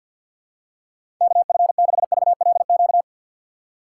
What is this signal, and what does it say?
Morse code, heard as a keyed tone, about 53 wpm, 700 Hz. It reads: KF6VLQ
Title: USA Call Signs - 6x Character Spacing - 50wpm